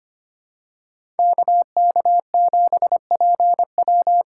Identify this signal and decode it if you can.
Morse code, heard as a keyed tone, about 25 wpm, 700 Hz. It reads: KK7PW
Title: USA Call Signs - 5 Characters - 4x Character Spacing - 25wpm